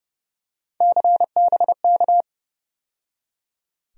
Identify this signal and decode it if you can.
Morse code, heard as a keyed tone, about 30 wpm, 700 Hz. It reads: CBK
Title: USA Call Sign Postfixes - 3 Letters 30wpm